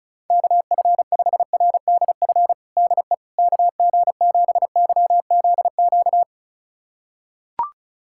Morse code, heard as a keyed tone, about 35 wpm, 700 Hz. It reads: KF5RDF DE KG7YZQ
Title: USA Callsign DE Callsign - 6 Characters - 35wpm